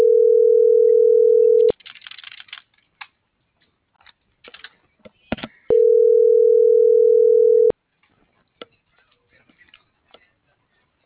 Ringback tone (North America), 440 + 480 Hz.